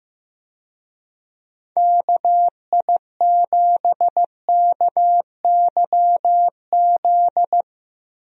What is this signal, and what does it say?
Morse code, heard as a keyed tone, about 15 wpm, 700 Hz. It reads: KI7KYZ
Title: USA Call Signs - 6 Characters - Repeated 2x - 15wpm